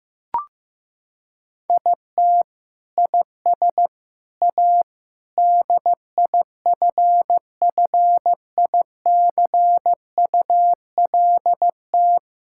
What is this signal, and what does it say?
Morse code, heard as a keyed tone, about 15 wpm, 700 Hz. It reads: IT IS A DIFFICULT
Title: Sentences from Top 1000 Words - Review Part 6 - 15wpm